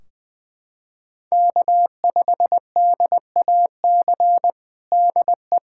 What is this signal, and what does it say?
Morse code, heard as a keyed tone, about 21 wpm, 700 Hz. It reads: K5DAC DE
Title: K1USN SST QSO Practice 20wpm